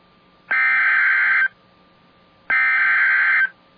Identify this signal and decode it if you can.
Emergency Alert System (SAME) tones (header bursts), heard as harsh buzzing bursts. It is heard from 0.5 to 3.4 s.